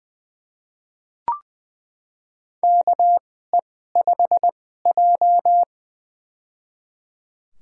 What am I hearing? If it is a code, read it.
Morse code, heard as a keyed tone, about 20 wpm, spaced as for 14 wpm, 700 Hz. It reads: KE5J
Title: USA Call Signs - 4 Characters - 2x Character Spacing - 20wpm